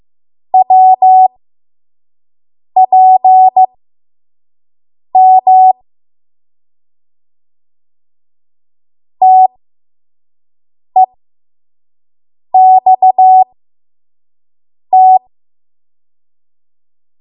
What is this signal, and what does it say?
Morse code, heard as a keyed tone, about 15 wpm, spaced as for 5 wpm, 750 Hz. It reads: WPM TEXT
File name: Morse / 05WPM